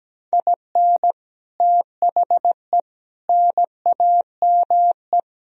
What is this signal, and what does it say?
Morse code, heard as a keyed tone, about 17 wpm, 700 Hz. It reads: IN THE NAME
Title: Sentences from Top 1000 Words - Review Part 3 - 17wpm